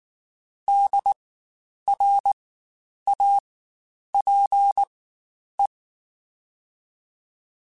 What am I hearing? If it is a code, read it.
Morse code, heard as a keyed tone, about 19 wpm, spaced as for 9 wpm, 800 Hz. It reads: DRAPE
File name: Spaced Alphas / Spaced Alphas mp3s / Ds sp @18